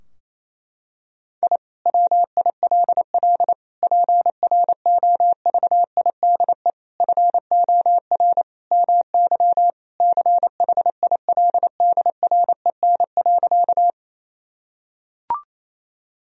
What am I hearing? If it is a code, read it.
Morse code, heard as a keyed tone, about 28 wpm, 700 Hz. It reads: I WILL PROVIDE FOR MY CHILDREN.
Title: Sentences from Top 1000 Words - Review Part 3 - 28wpm